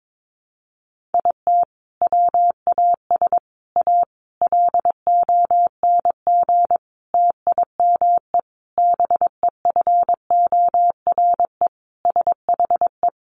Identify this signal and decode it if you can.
Morse code, heard as a keyed tone, about 22 wpm, 700 Hz. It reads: IT WAS A LONG TIME BEFORE SHE
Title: Sentences from Top 500 Words 22wpm